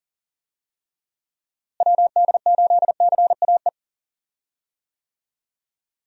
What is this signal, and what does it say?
Morse code, heard as a keyed tone, about 39 wpm, 700 Hz. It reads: WD8CAE
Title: USA Call Signs - 6 Characters - Repeated 3x - 40wpm